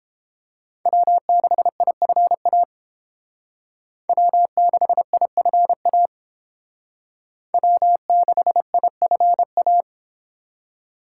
Morse code, heard as a keyed tone, about 30 wpm, 700 Hz. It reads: W6IFA W6IFA W6IFA